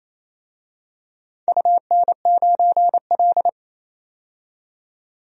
Morse code, heard as a keyed tone, about 29 wpm, 700 Hz. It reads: UN9L
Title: DXCC-100 Call Signs - 4 Characters - 28wpm